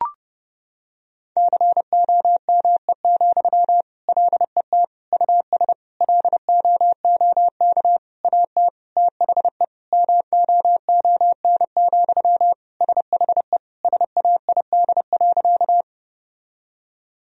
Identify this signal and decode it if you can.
Morse code, heard as a keyed tone, about 30 wpm, 700 Hz. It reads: COME, LET US LOOK AT THE MOON, SHE SAID.
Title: Sentences from Top 1000 Words - Review Part 3 - 30wpm